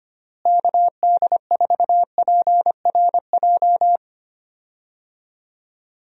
Morse code, heard as a keyed tone, about 25 wpm, 700 Hz. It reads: KD4PRJ
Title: USA Call Signs - 6 Characters - Repeated 2x - 25wpm